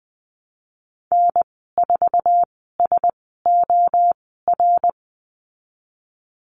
Morse code, heard as a keyed tone, about 20 wpm, spaced as for 15 wpm, 700 Hz. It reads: N4SOR
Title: 076 Lesson 13g - Letter L - Call Signs 20wpm